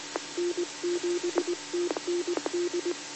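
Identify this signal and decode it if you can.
Morse code, heard as a keyed tone, about 20 wpm, 350 Hz, with noise. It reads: N7TND
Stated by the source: Morse code received on the 2 m band at 146.115 MHz.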